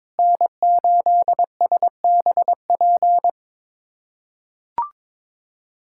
Morse code, heard as a keyed tone, about 21 wpm, 700 Hz. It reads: N8SBP
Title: USA Callsign DE Callsign 22wpm